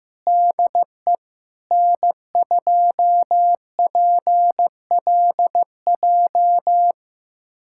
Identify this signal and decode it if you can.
Morse code, heard as a keyed tone, about 16 wpm, 700 Hz. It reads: DE N2PLJ
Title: USA Callsign DE Callsign 15wpm